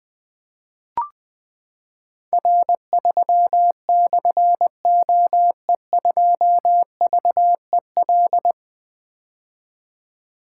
Morse code, heard as a keyed tone, about 20 wpm, 700 Hz. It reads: R3/OE2VEL
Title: Difficult Call Signs 20wpm